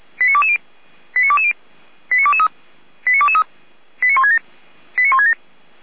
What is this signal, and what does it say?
Five-tone selective call (ZVEI), heard as quick runs of five tones, tones of about 70 ms. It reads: 89220 89220 89223 89223 89167 89167